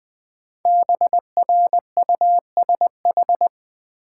Morse code, heard as a keyed tone, about 21 wpm, 700 Hz. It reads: BRUSH
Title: Sentences from Top 1100 Words - New Word Drill Alt - 20wpm